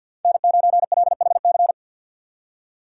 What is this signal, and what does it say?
Morse code, heard as a keyed tone, about 46 wpm, 700 Hz. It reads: N9LHC
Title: USA Call Signs - 5 Characters - 2x Character Spacing - 50wpm